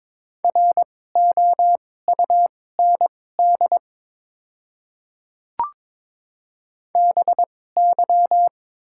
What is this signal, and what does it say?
Morse code, heard as a keyed tone, about 22 wpm, spaced as for 16 wpm, 700 Hz. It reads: ROUND BY
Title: 2x Character Spacing - Top 200 Words - 22wpm